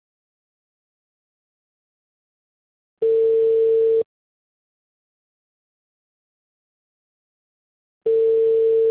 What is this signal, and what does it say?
Ringback tone (China, 450 Hz).